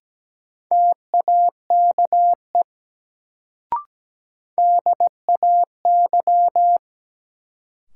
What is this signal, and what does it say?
Morse code, heard as a keyed tone, about 17 wpm, 700 Hz. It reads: TAKE DAY
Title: Sentences from Top 200 Words - New Word Drill - 17wpm